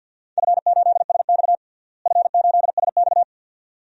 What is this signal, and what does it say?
Morse code, heard as a keyed tone, about 50 wpm, 700 Hz. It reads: F8SX F8SX
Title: DXCC-100 Call Signs - 4 Characters - Repeated 2x - 50wpm